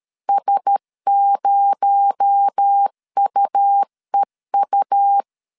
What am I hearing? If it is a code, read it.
Morse code, heard as a keyed tone, about 13 wpm, 800 Hz. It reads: S0UEU
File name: CW / DET / 1 6 1500C S C 12wpm 3533